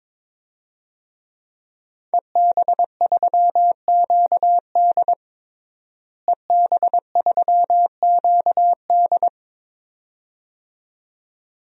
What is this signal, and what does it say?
Morse code, heard as a keyed tone, about 22 wpm, 700 Hz. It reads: EB3QD EB3QD
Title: DXCC-100 Call Signs - 5 Characters - Repeated 2x - 22wpm